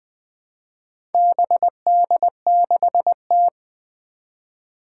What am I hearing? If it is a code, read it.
Morse code, heard as a keyed tone, about 20 wpm, 700 Hz. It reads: BD6T
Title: Four Challenging Letter-Number Combinations 20wpm